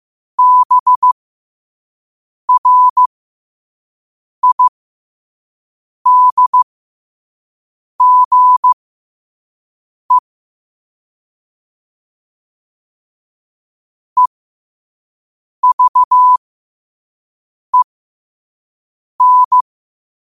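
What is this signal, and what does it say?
Morse code, heard as a keyed tone, about 15 wpm, spaced as for 5 wpm, 1000 Hz. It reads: BRIDGE EVEN
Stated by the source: Morse at 5 wpm.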